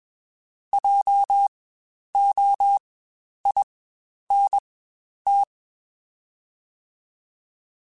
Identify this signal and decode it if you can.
Morse code, heard as a keyed tone, about 21 wpm, spaced as for 10 wpm, 800 Hz. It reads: JOINT
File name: Spaced Alphas / Spaced Alphas mp3s / Js sp @20